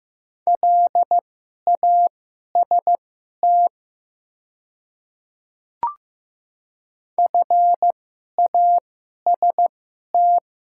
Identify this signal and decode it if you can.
Morse code, heard as a keyed tone, about 15 wpm, spaced as for 11 wpm, 700 Hz. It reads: LAST FAST
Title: 2x Character Spacing - Top 500 Words - 15wpm